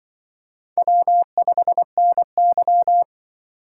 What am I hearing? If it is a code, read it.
Morse code, heard as a keyed tone, about 25 wpm, 700 Hz. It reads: W5NY
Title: Speed Racing - USA Call Signs 20wpm